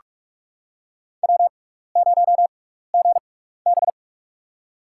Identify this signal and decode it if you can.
Morse code, heard as a keyed tone, about 47 wpm, spaced as for 16 wpm, 700 Hz. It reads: W0GB
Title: USA Call Signs - 6x Character Spacing - 45wpm